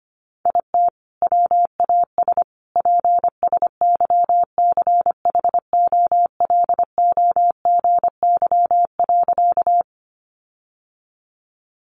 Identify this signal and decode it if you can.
Morse code, heard as a keyed tone, about 25 wpm, 700 Hz. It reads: IT WAS PSYCHOLOGY.
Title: Sentences from Top 1400 Words 25wpm